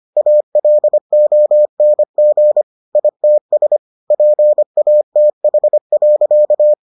Morse code, heard as a keyed tone, about 25 wpm, 600 Hz. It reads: ALONG ITS PATH.